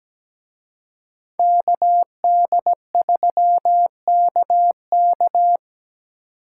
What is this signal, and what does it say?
Morse code, heard as a keyed tone, about 20 wpm, 700 Hz. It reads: KD3KK
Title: Speed Racing - USA 2x2 Call Signs 17wpm